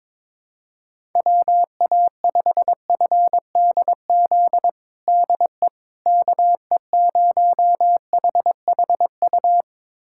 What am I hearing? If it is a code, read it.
Morse code, heard as a keyed tone, about 22 wpm, 700 Hz. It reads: WA5FDZ DE KE0HHU
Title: USA Callsign DE Callsign 22wpm